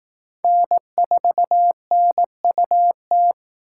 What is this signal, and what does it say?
Morse code, heard as a keyed tone, about 19 wpm, 700 Hz. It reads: N4NUT
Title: Speed Racing - USA 1x3 Call Signs 15wpm